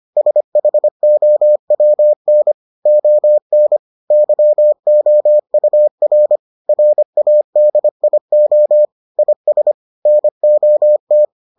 Morse code, heard as a keyed tone, about 25 wpm, 600 Hz. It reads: SHOWN ON YOUR RADIO IS NOT